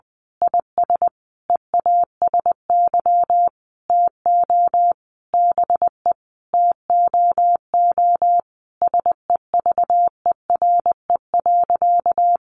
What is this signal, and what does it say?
Morse code, heard as a keyed tone, about 20 wpm, 700 Hz. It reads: IS EASY TO BE TOO SEVERE.
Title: Sentences from Top 1400 Words 20wpm